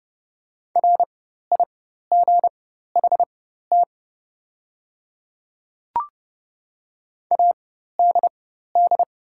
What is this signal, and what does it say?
Morse code, heard as a keyed tone, about 32 wpm, spaced as for 14 wpm, 700 Hz. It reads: RIGHT ADD
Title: 4x Character Spacing - Top 200 Words - 30wpm